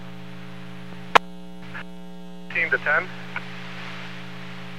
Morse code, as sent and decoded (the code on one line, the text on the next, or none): none